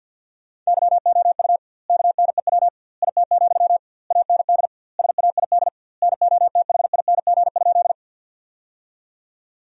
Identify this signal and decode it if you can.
Morse code, heard as a keyed tone, about 46 wpm, 700 Hz. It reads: YOU KNEW IT, AND SAID NOTHING?
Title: 2x Word-Spacing - Sentences from Top 500 Words - 50wpm